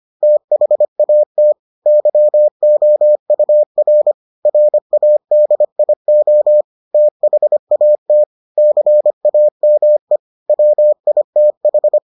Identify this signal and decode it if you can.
Morse code, heard as a keyed tone, about 25 wpm, 600 Hz. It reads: THAT YOUR RADIO THAT CAME WITH